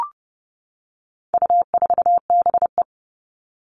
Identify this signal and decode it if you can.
Morse code, heard as a keyed tone, about 30 wpm, 700 Hz. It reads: U4BE
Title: All Cut Numbers - 4-Digit 30wpm